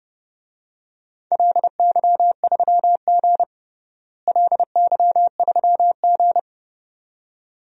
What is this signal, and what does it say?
Morse code, heard as a keyed tone, about 30 wpm, 700 Hz. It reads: LY3G LY3G
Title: DXCC-100 Call Signs - 4 Characters - Repeated 2x - 30wpm